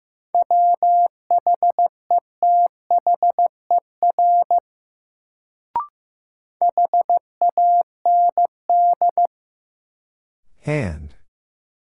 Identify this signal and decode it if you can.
Morse code, heard as a keyed tone, about 15 wpm, 700 Hz. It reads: WHETHER HAND
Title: Sets of 3 Words - New Word Drill - 15wpm